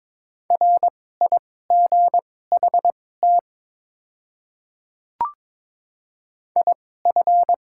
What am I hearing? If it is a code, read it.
Morse code, heard as a keyed tone, about 22 wpm, spaced as for 16 wpm, 700 Hz. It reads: RIGHT IF
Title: 2x Character Spacing - Top 200 Words - 22wpm